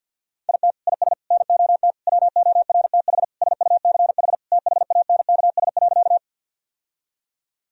Morse code, heard as a keyed tone, about 46 wpm, 700 Hz. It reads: IT IS NOT WORTH SUCH THANKS.
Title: Sentences from Top 1000 Words - Review Part 3 - 50wpm